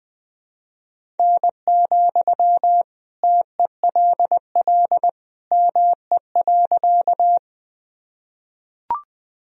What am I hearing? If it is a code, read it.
Morse code, heard as a keyed tone, about 21 wpm, 700 Hz. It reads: N, TELL ME.
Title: Sentences from Top 200 Words and Abbreviations 20wpm